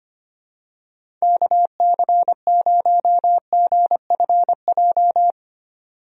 Morse code, heard as a keyed tone, about 25 wpm, 700 Hz. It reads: KC0GFJ
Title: USA Call Signs - 6 Characters - 25wpm